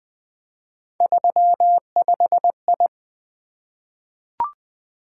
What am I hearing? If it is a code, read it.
Morse code, heard as a keyed tone, about 20 wpm, 700 Hz. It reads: 35I
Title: Three Letter-Number Combinations - 20wpm